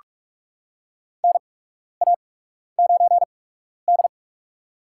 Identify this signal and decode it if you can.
Morse code, heard as a keyed tone, about 43 wpm, spaced as for 12 wpm, 700 Hz. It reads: NA9D
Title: USA Call Signs - 4 Characters - 8x Character Spacing - 45wpm